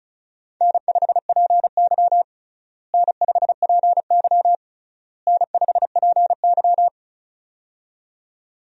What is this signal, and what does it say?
Morse code, heard as a keyed tone, about 36 wpm, 700 Hz. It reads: N5PY N5PY N5PY
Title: USA Call Signs - 4 Characters - Repeated 3x - 35wpm